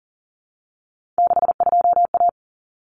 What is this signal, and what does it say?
Morse code, heard as a keyed tone, about 40 wpm, 700 Hz. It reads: 62A
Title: Common Cut Numbers - 3-Digit 40wpm